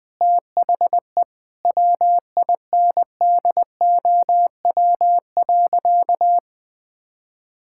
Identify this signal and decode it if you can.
Morse code, heard as a keyed tone, about 20 wpm, 700 Hz. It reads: THE WINDOW.
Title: Sentences from Top 1600 Words 20wpm